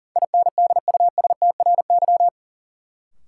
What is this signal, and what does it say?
Morse code, heard as a keyed tone, about 37 wpm, 700 Hz. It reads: INDUSTRY
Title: Sentences from Top 1000 Words - Review Part 4 - New Word Drill - 40wpm